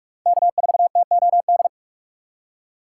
Morse code, heard as a keyed tone, about 43 wpm, 700 Hz. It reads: K4TOD
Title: USA Call Signs - 5 Characters - 3x Character Spacing - 45wpm